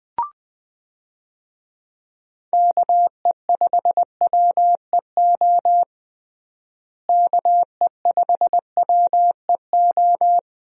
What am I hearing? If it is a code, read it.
Morse code, heard as a keyed tone, about 20 wpm, 700 Hz. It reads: KE5WEO KE5WEO
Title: USA Call Signs - 6 Characters - Repeated 2x - 20wpm